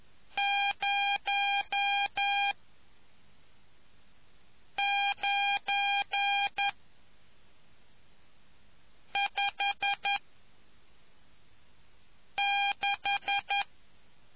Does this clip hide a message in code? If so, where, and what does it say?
Yes: Morse code from 0.4 to 13.6 s, heard as a keyed tone, about 11 wpm, 800 Hz. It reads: 0 9 5 6